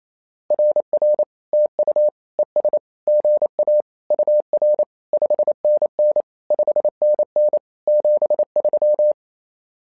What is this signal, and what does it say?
Morse code, heard as a keyed tone, about 28 wpm, 600 Hz. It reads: RR TU ES GA UR 5NN 5NN 73
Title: SOTA QSO Practice 28wpm